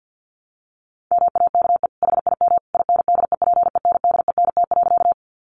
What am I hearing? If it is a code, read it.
Morse code, heard as a keyed tone, about 50 wpm, 700 Hz. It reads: MAKE HIM INDEPENDENT.
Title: Sentences from Top 900 Words 50wpm